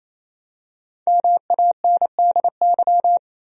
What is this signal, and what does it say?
Morse code, heard as a keyed tone, about 28 wpm, 700 Hz. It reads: MANDY
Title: USA Common Names - 28wpm